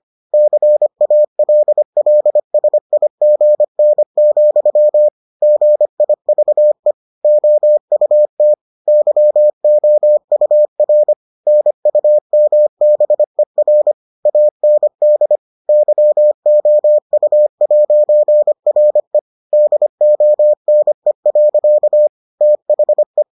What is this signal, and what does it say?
Morse code, heard as a keyed tone, about 25 wpm, 600 Hz. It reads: CALLSIGN, GIVE OUT YOUR NUMBER AND YOU'RE DONE. THE